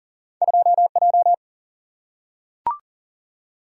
Morse code, heard as a keyed tone, about 38 wpm, 700 Hz. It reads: 2J
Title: Single-Letter Single-Number 40wpm